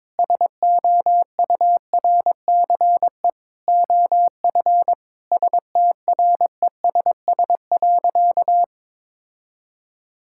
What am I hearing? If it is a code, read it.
Morse code, heard as a keyed tone, about 22 wpm, 700 Hz. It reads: SOURCE OF STRESS.
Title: Sentences from Top 1000 Words - Review Part 6 - 22wpm